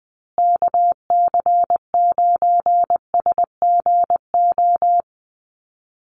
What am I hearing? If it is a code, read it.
Morse code, heard as a keyed tone, about 20 wpm, 700 Hz. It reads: KC9SGO
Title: USA Call Signs Repeated 3x 20wpm